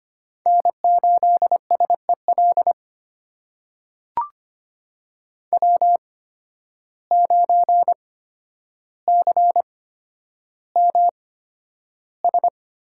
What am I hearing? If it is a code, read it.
Morse code, heard as a keyed tone, about 25 wpm, spaced as for 11 wpm, 700 Hz. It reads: N8SEL W9CMS
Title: USA Call Signs - 5 Characters - 8x Character Spacing - 25wpm